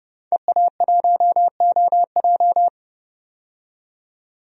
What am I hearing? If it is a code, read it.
Morse code, heard as a keyed tone, about 30 wpm, 700 Hz. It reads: EA1OJ
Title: DXCC-100 Call Signs - 5 Characters - 30wpm